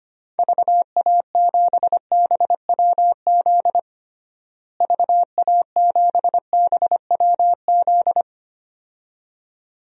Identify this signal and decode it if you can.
Morse code, heard as a keyed tone, about 25 wpm, 700 Hz. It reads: VA7BWZ VA7BWZ
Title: DXCC-100 Call Signs - 6 Characters - Repeated 2x - 25wpm